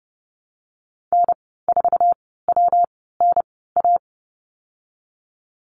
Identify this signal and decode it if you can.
Morse code, heard as a keyed tone, about 30 wpm, spaced as for 17 wpm, 700 Hz. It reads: N4WNA
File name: course / 30:10wpm / 116 Lesson 19g - Letter w - Call Signs 30wpm